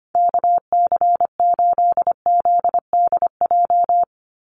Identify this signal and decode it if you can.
Morse code, heard as a keyed tone, about 25 wpm, 700 Hz. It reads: KC8ZDJ
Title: USA Call Signs 25wpm